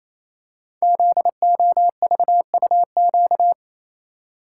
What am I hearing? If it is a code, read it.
Morse code, heard as a keyed tone, about 27 wpm, 700 Hz. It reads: ZOVUQ
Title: Code Groups - 5 Letters - 28wpm